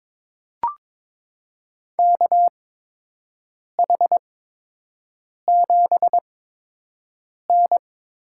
Morse code, heard as a keyed tone, about 22 wpm, spaced as for 6 wpm, 700 Hz. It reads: KH7N
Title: USA Call Signs - 8x Character Spacing - 22wpm